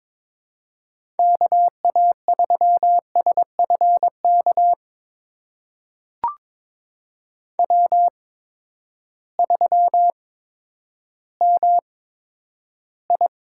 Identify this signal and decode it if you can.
Morse code, heard as a keyed tone, about 22 wpm, spaced as for 11 wpm, 700 Hz. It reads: KA3SFK W3MI
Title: USA Call Signs - 8x Character Spacing - 22wpm